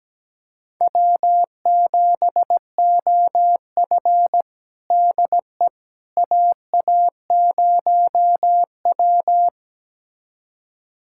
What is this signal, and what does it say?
Morse code, heard as a keyed tone, about 17 wpm, 700 Hz. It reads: W7OF DE AA0W
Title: USA Callsign DE Callsign - 4 Characters - 17wpm